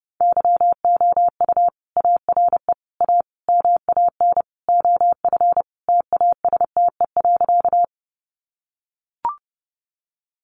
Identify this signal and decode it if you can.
Morse code, heard as a keyed tone, about 30 wpm, 700 Hz. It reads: YOU ARE A MAN OF TASTE.
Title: Sentences from Top 1100 Words 30wpm